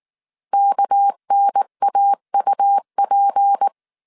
Morse code, heard as a keyed tone, about 19 wpm, 800 Hz. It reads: KNAUP
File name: CW / DET / 1 6 1500C S C 18wpm 2356